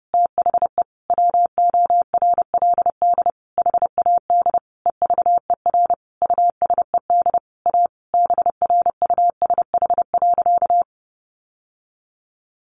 Morse code, heard as a keyed tone, about 30 wpm, 700 Hz. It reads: THE WORLD HAD EVER USED A BRUSH.
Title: Sentences from Top 1100 Words 30wpm